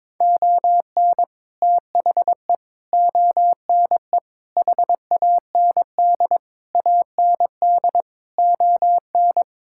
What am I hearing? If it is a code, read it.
Morse code, heard as a keyed tone, about 22 wpm, 700 Hz. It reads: ON THE ONE HAND AND ON
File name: practice / 22wpm / Sets of 8 Words - 22wpm